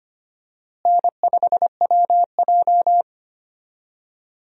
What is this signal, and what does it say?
Morse code, heard as a keyed tone, about 25 wpm, 700 Hz. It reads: N5WJ
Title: USA Call Signs - 4 Characters - Repeated 2x - 25wpm